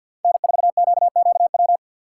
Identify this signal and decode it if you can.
Morse code, heard as a keyed tone, about 52 wpm, 700 Hz. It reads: N4QQW
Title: USA Call Signs - 5 Characters - 4x Character Spacing - 50wpm